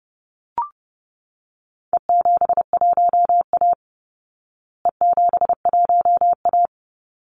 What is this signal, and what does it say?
Morse code, heard as a keyed tone, about 28 wpm, 700 Hz. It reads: E71A E71A
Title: Call Signs Repeated 3x - 30wpm